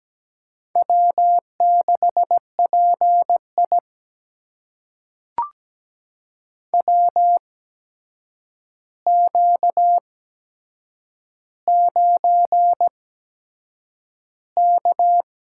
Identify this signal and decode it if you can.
Morse code, heard as a keyed tone, about 17 wpm, spaced as for 7 wpm, 700 Hz. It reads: W6PI WQ9K
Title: USA Call Signs - 4 Characters - 8x Character Spacing - 17wpm